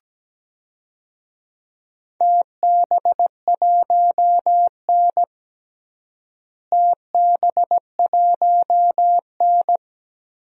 Morse code, heard as a keyed tone, about 17 wpm, 700 Hz. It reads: TB1N TB1N
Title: DXCC-100 Call Signs - 4 Characters - Repeated 3x - 17wpm